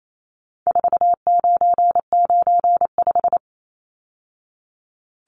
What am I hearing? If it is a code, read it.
Morse code, heard as a keyed tone, about 28 wpm, 700 Hz. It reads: 4995
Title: Numbers - 4-Digit 28wpm